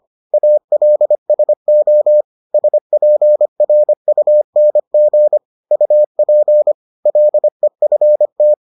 Morse code, heard as a keyed tone, about 25 wpm, 600 Hz. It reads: ALSO SPRUNG UP LEFT